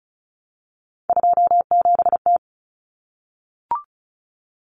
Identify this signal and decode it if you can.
Morse code, heard as a keyed tone, about 34 wpm, 700 Hz. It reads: 27T
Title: Common Cut Numbers - 3-Digit 35wpm